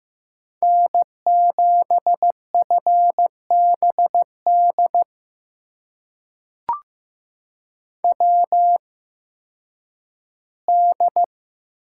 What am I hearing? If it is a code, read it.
Morse code, heard as a keyed tone, about 15 wpm, spaced as for 10 wpm, 700 Hz. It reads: N7FBD WD
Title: USA Call Signs - 8x Character Spacing - 15wpm